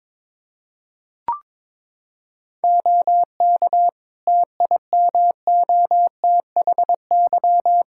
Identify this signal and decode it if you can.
Morse code, heard as a keyed tone, about 22 wpm, 700 Hz. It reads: OK TIMOTHY
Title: Practice QSO - Exchange 2 - Info - 22wpm